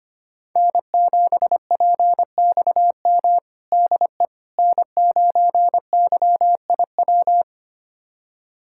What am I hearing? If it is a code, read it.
Morse code, heard as a keyed tone, about 25 wpm, 700 Hz. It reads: N7PXM DE N9YIW
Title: USA Callsign DE Callsign - 5 Characters - 25wpm